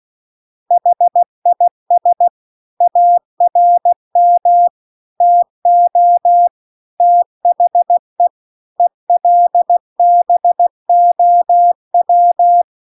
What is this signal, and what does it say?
Morse code, heard as a keyed tone, about 16 wpm, 700 Hz. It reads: HIS ARM TO THE ELBOW
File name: CW-The-Big-Twoheaded-River-2-16WPM